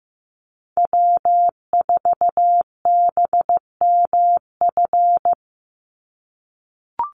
Morse code, heard as a keyed tone, about 15 wpm, 700 Hz. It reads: W4BMF
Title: USA Call Signs 15wpm